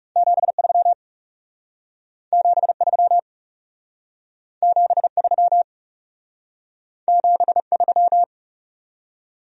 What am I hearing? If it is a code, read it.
Morse code, heard as a keyed tone, about 40 wpm, 700 Hz. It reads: 73 73 73 73